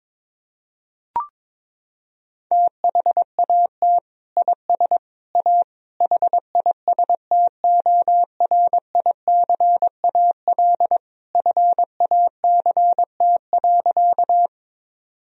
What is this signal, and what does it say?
Morse code, heard as a keyed tone, about 22 wpm, 700 Hz. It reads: THAT IS A HISTORICAL FACT.
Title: Sentences from Top 1000 Words - Review Part 4 - 22wpm